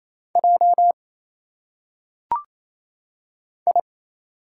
Morse code, heard as a keyed tone, about 27 wpm, 700 Hz. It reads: J I
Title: Single Letters - Phonetic Pronunciation 28wpm